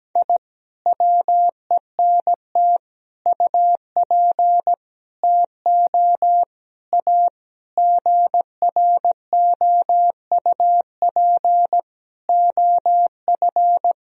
Morse code, heard as a keyed tone, about 17 wpm, 700 Hz. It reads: I WENT UP TO A GROUP OF
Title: Sentences from Top 1000 Words - Review Part 1 - 17wpm